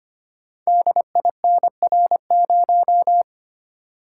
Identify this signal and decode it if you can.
Morse code, heard as a keyed tone, about 25 wpm, 700 Hz. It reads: DINR0
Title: Code Groups - 5 Letter:Numbers - 25wpm